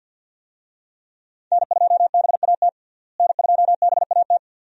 Morse code, heard as a keyed tone, about 51 wpm, 700 Hz. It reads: N2BAT N2BAT
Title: USA Call Signs - 5 Characters - Repeated 2x - 50wpm